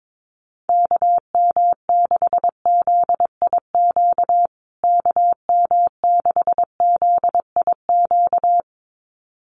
Morse code, heard as a keyed tone, about 22 wpm, 700 Hz. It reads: KM6ZIQ KM6ZIQ
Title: USA Call Signs Repeated 2x 22wpm